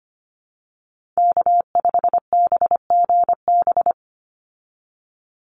Morse code, heard as a keyed tone, about 25 wpm, 700 Hz. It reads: K5BGB
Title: Call Signs Repeated 3x - 25wpm